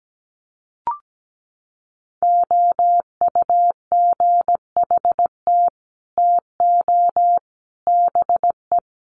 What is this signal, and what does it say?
Morse code, heard as a keyed tone, about 17 wpm, 700 Hz. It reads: OUGHT TO BE
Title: Sets of 5 Words Encore2 17wpm